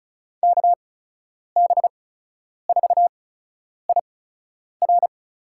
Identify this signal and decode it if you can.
Morse code, heard as a keyed tone, about 34 wpm, spaced as for 10 wpm, 700 Hz. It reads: KB4IR
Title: USA Call Signs - 8x Character Spacing - 35wpm